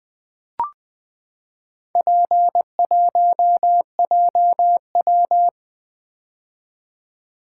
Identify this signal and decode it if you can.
Morse code, heard as a keyed tone, about 20 wpm, 700 Hz. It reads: P1JW
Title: Four Letter-Number Combinations - 20wpm